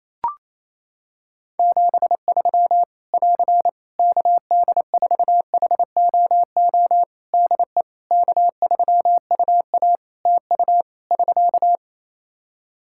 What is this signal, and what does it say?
Morse code, heard as a keyed tone, about 28 wpm, 700 Hz. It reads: 73 <AR> KD4HOO DE K3UA TU <SK>
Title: Practice QSO - Exchange 4 - Ending - 28wpm